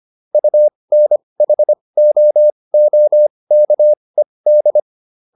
Morse code, heard as a keyed tone, about 25 wpm, spaced as for 20 wpm, 600 Hz. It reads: UNHOOKED